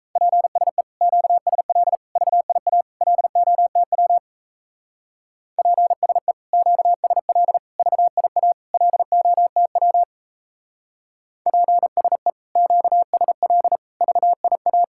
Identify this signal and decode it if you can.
Morse code, heard as a keyed tone, about 35 wpm, 700 Hz. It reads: PSE QSL VIA LOTW PSE QSL VIA LOTW PSE QSL VIA
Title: Speed-Racing Common QSO Phrases Encore - 28wpm